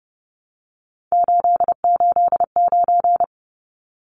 Morse code, heard as a keyed tone, about 30 wpm, 700 Hz. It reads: 889
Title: Numbers - 3-Digit 30wpm